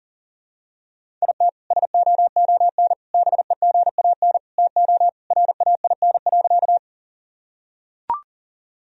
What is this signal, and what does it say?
Morse code, heard as a keyed tone, about 40 wpm, 700 Hz. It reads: IT SOON BEGAN TO RAIN.
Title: Sentences from Top 1000 Words - Review Part 1 - 40wpm